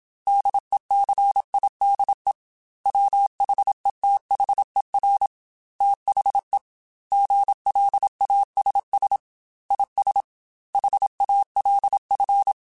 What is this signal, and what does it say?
Morse code, heard as a keyed tone, about 27 wpm, 800 Hz. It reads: DECIDE WHETHER THE GLASS IS HALF